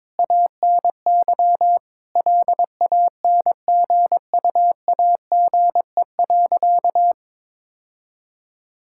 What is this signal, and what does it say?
Morse code, heard as a keyed tone, about 22 wpm, 700 Hz. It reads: ANY LANGUAGE.
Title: Sentences from Top 1600 Words 22wpm